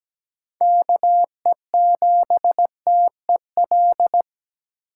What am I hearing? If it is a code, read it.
Morse code, heard as a keyed tone, about 17 wpm, 700 Hz. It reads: KE7TEL
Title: USA Call Signs - 6 Characters - 8x Character Spacing - 17wpm